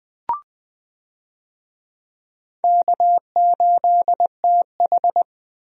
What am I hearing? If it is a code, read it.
Morse code, heard as a keyed tone, about 20 wpm, 700 Hz. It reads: K8TH
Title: USA Call Signs - 4 Characters - Repeated 3x - 20wpm